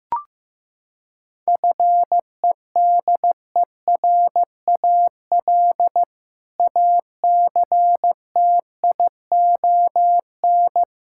Morse code, heard as a keyed tone, about 15 wpm, 700 Hz. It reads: FEDERAL ACTION
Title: Sentences from Top 1000 Words - Review Part 5 - 15wpm